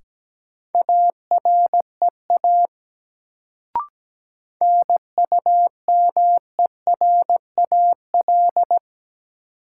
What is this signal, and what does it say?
Morse code, heard as a keyed tone, about 17 wpm, 700 Hz. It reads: AREA NUMERAL
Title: Sentences from Top 1000 Words - Review Part 2 - New Word Drill - 17wpm